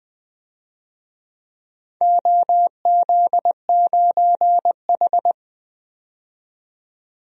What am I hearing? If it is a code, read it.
Morse code, heard as a keyed tone, about 20 wpm, 700 Hz. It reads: OZ9H
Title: DXCC-100 Call Signs - 4 Characters - 20wpm